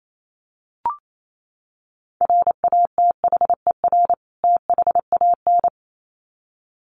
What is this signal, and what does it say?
Morse code, heard as a keyed tone, about 28 wpm, 700 Hz. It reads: RATHER THAN
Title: Sets of 2 Words 28wpm